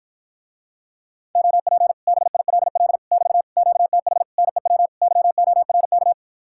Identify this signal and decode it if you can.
Morse code, heard as a keyed tone, about 53 wpm, 700 Hz. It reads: OP BILL <BT> QTH NEW YORK